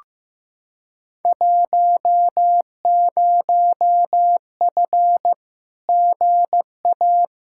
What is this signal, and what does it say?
Morse code, heard as a keyed tone, about 15 wpm, 700 Hz. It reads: 10F GA
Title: Field Day Exchanges Encore - 15wpm